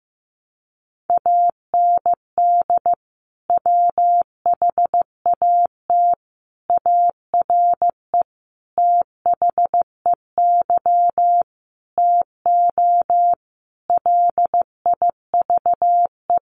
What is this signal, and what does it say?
Morse code, heard as a keyed tone, about 14 wpm, 700 Hz. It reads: AND WHAT ARE THEY TO LIVE
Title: Sentences from Top 1100 Words 15wpm